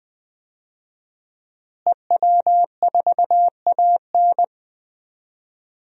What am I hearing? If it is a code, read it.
Morse code, heard as a keyed tone, about 21 wpm, 700 Hz. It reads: EW4AN
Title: DXCC-100 Call Signs - 5 Characters - Repeated 3x - 20wpm